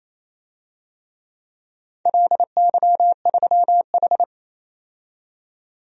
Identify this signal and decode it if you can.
Morse code, heard as a keyed tone, about 28 wpm, 700 Hz. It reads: LY3H
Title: DXCC-100 Call Signs - 4 Characters - 28wpm